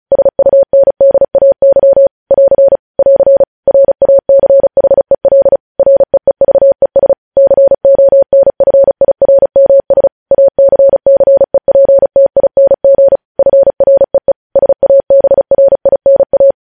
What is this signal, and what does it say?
Morse code, heard as a keyed tone, about 36 wpm, 550 Hz. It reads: SUNDAY <AR> <AR> RACHEL REEVES CONFIRMS ACCEPTING FREE SABRINA
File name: podcast / News headlines-1x-35x35